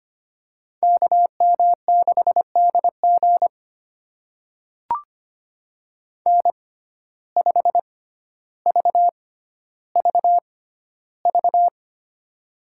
Morse code, heard as a keyed tone, about 25 wpm, spaced as for 13 wpm, 700 Hz. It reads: KM6DG N5VVV
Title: USA Call Signs - 5 Characters - 6x Character Spacing - 25wpm